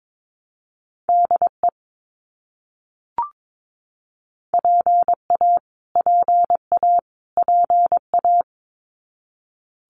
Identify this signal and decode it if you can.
Morse code, heard as a keyed tone, about 22 wpm, 700 Hz. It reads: DE PA PA PA
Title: US States Repeated 3x 22wpm